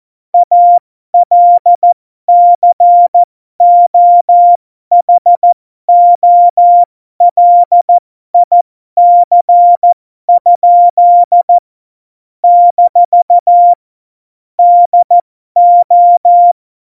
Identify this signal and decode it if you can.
Morse code, heard as a keyed tone, about 14 wpm, spaced as for 12 wpm, 700 Hz. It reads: ALCOHOLIC? - DO
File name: Morsekurs / Witze